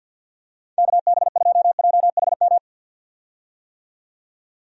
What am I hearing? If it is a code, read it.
Morse code, heard as a keyed tone, about 52 wpm, 700 Hz. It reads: KB2JHM
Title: USA Call Signs - 6 Characters - 50wpm